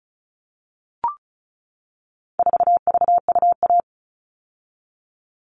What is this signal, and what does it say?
Morse code, heard as a keyed tone, about 34 wpm, 700 Hz. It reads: 4VUA
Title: All Cut Numbers - 4-Digit 35wpm